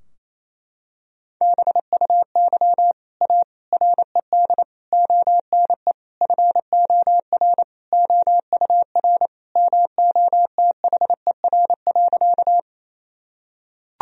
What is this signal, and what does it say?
Morse code, heard as a keyed tone, about 28 wpm, 700 Hz. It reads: BUY A RED ONE FOR OUR MOTHER.
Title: Sentences from Top 1000 Words - Review Part 2 - 28wpm